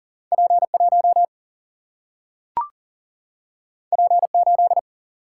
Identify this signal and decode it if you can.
Morse code, heard as a keyed tone, about 40 wpm, 700 Hz. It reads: P 1 P 8
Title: Single-Letter Single-Number 40wpm